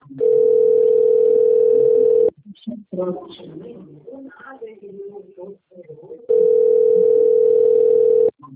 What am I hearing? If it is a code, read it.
Ringback tone (North America), 440 + 480 Hz, with speech.
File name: rg-1600-unknown- - - .88076